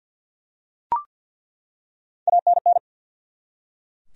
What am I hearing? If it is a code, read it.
Morse code, heard as a keyed tone, about 48 wpm, 700 Hz. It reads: ANN
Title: USA Common Names - Max 3 Letters - 50wpm